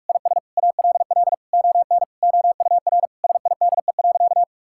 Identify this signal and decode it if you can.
Morse code, heard as a keyed tone, about 46 wpm, 700 Hz. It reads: IS ALL ON OUR SIDE.
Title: Sentences from Top 1000 Words - Review Part 6 - 45wpm